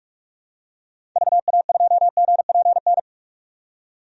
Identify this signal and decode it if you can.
Morse code, heard as a keyed tone, about 46 wpm, 700 Hz. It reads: VA2GPN
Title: DXCC-100 Call Signs - 6 Characters - 45wpm